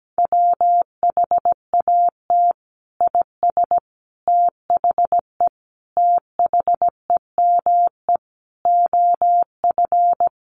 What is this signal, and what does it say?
Morse code, heard as a keyed tone, about 17 wpm, 700 Hz. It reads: WHAT IS THE THEME OF
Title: Sentences from Top 1100 Words 17wpm